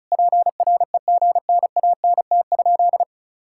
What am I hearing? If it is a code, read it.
Morse code, heard as a keyed tone, about 34 wpm, 700 Hz. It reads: PREGNANT?
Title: Sentences from Top 1000 Words - Review Part 6 - 35wpm